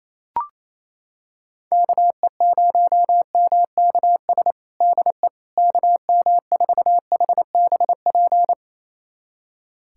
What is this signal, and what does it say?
Morse code, heard as a keyed tone, about 28 wpm, 700 Hz. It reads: KE0MKS DE KM4HBP
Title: USA Callsign DE Callsign - 6 Characters - 28wpm